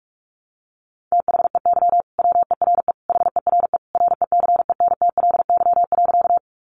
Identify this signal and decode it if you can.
Morse code, heard as a keyed tone, about 46 wpm, 700 Hz. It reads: THEY WERE HERE RECENTLY.
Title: Sentences from Top 1200 Words 45wpm